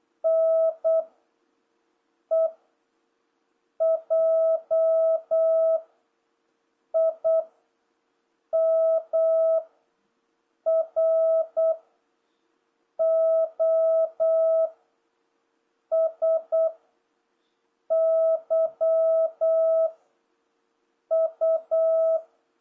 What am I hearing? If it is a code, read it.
Morse code, heard as a keyed tone, about 8 wpm, 650 Hz. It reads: N E J I M R O S Y U